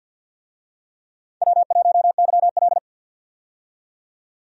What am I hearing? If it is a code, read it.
Morse code, heard as a keyed tone, about 46 wpm, 700 Hz. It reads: W1YL
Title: USA Call Signs - 4 Characters - 50wpm